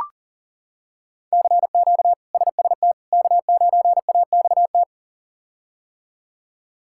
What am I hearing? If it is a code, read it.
Morse code, heard as a keyed tone, about 37 wpm, 700 Hz. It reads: CQ SST K9AXT
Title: K1USN SST QSO Practice - 40wpm